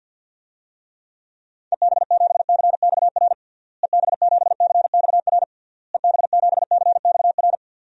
Morse code, heard as a keyed tone, about 50 wpm, 700 Hz. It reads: EB7CXR EB7CXR EB7CXR
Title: DXCC-100 Call Signs - 6 Characters - Repeated 3x - 50wpm